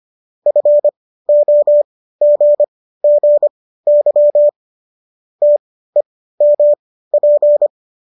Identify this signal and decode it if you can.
Morse code, heard as a keyed tone, about 25 wpm, spaced as for 15 wpm, 590 Hz. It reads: FOGGY TEMP